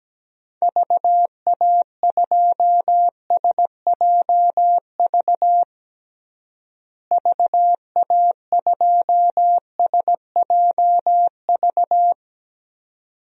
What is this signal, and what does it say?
Morse code, heard as a keyed tone, about 17 wpm, 700 Hz. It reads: VA2SJV VA2SJV
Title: DXCC-100 Call Signs - 6 Characters - Repeated 2x - 17wpm